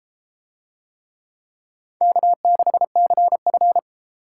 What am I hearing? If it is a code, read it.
Morse code, heard as a keyed tone, about 30 wpm, 700 Hz. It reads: K6CF